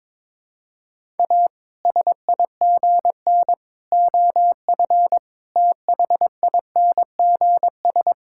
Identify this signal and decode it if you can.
Morse code, heard as a keyed tone, about 22 wpm, 700 Hz. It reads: A SIGN OF THINGS
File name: practice / 22wpm / Sets of 6 Words - 22wpm